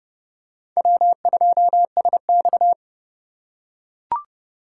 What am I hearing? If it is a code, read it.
Morse code, heard as a keyed tone, about 30 wpm, 700 Hz. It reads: W2SX
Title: USA Call Signs - 4x Character Spacing - 30wpm